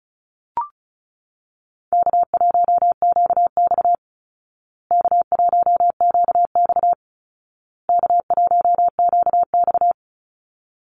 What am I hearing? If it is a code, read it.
Morse code, heard as a keyed tone, about 34 wpm, 700 Hz. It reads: K1QX K1QX K1QX
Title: Easy Call Signs Repeated 3x - 35wpm